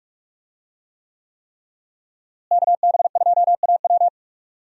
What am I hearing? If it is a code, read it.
Morse code, heard as a keyed tone, about 46 wpm, 700 Hz. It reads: KB2AW
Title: USA Call Signs - 5 Characters - Repeated 3x - 45wpm